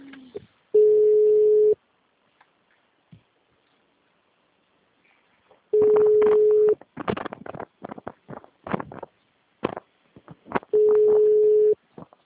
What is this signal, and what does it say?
Ringback tone (Europe, 425 Hz), with speech.
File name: out- -2035- - - .2